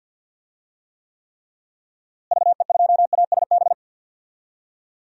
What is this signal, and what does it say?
Morse code, heard as a keyed tone, about 52 wpm, 700 Hz. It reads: VE2ASB
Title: DXCC-100 Call Signs - 6 Characters - Repeated 3x - 50wpm